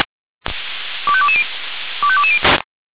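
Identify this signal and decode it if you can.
Five-tone selective call (ZVEI), heard as quick runs of five tones, tones of about 70 ms. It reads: 25220 25220
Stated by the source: Five-tone ZVEI selective call received on an SDR.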